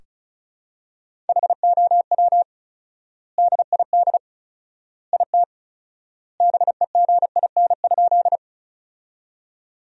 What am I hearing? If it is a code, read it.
Morse code, heard as a keyed tone, about 34 wpm, 700 Hz. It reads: HOW DID IT BEGIN?
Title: 4x Word-Spacing - Sentences from Top 300 Words - 35wpm